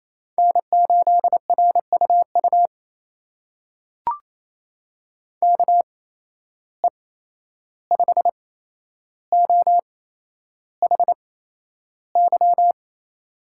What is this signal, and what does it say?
Morse code, heard as a keyed tone, about 28 wpm, spaced as for 11 wpm, 700 Hz. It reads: N8RUU KE5OHY
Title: USA Call Signs - 8x Character Spacing - 28wpm